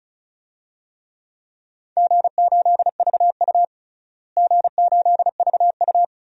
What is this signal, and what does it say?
Morse code, heard as a keyed tone, about 34 wpm, 700 Hz. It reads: G8VU G8VU
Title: DXCC-100 Call Signs - 4 Characters - Repeated 2x - 35wpm